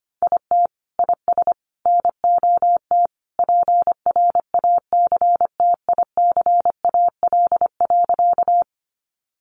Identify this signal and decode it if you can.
Morse code, heard as a keyed tone, about 25 wpm, 700 Hz. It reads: IT IS NOT PRACTICAL.
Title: Sentences from Top 1400 Words 25wpm